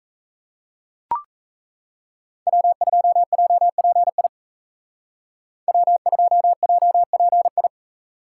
Morse code, heard as a keyed tone, about 34 wpm, 700 Hz. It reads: W2JPI W2JPI
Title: Speed Racing - USA 1x3 Call Signs 28wpm